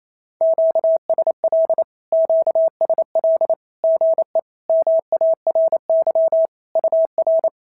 Morse code, heard as a keyed tone, about 28 wpm, 650 Hz. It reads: QSL QSL GE MARY UR
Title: SOTA QSO Practice 28wpm